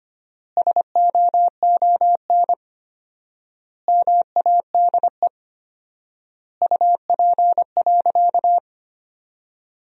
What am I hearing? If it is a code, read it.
Morse code, heard as a keyed tone, about 25 wpm, 700 Hz. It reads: SOON MADE UP.
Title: 4x Word-Spacing - Sentences from Top 500 Words - 25wpm